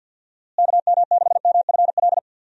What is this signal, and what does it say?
Morse code, heard as a keyed tone, about 50 wpm, 700 Hz. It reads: KM6MFL
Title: USA Call Signs - 6 Characters - Repeated 2x - 50wpm